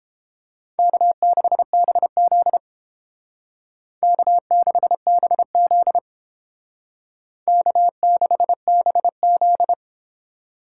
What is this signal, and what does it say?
Morse code, heard as a keyed tone, about 30 wpm, 700 Hz. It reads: K6BZ K6BZ K6BZ